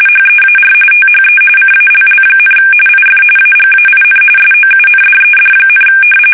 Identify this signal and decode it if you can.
RTTY, about 45.45 baud. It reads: HEAR THE CALL, IF AGAIN SOUNDED,
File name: fsk850 / battleshipnj